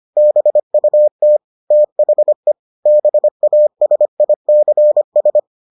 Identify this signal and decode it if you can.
Morse code, heard as a keyed tone, about 25 wpm, 600 Hz. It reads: BUT THE BASICS